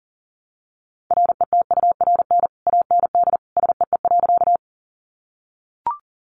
Morse code, heard as a keyed tone, about 40 wpm, 700 Hz. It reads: RETURN AND SEE.
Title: Sentences from Top 800 Words 40wpm